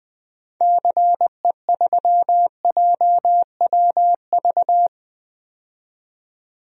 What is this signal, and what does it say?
Morse code, heard as a keyed tone, about 20 wpm, 700 Hz. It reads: CE3JWV
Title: DXCC-100 Call Signs - 6 Characters - 20wpm